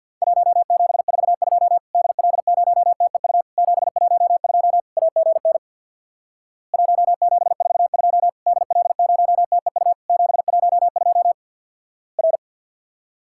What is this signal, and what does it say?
Morse code, heard as a keyed tone, about 50 wpm, 700 Hz. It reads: 1742 DL0TEU 712 AGN 1742 DL0TEU 712 R
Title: Worked All Europe DX Contest - QTC Exchange with Repeats - 50wpm